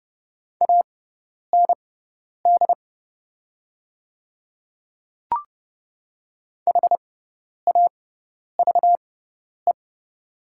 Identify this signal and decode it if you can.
Morse code, heard as a keyed tone, about 30 wpm, spaced as for 10 wpm, 700 Hz. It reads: AND HAVE
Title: 6x Character Spacing - Top 100 Words - 30wpm